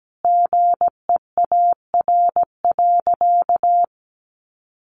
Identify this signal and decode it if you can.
Morse code, heard as a keyed tone, about 17 wpm, 700 Hz. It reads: GEAR.
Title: Sentences from Top 1400 Words 17wpm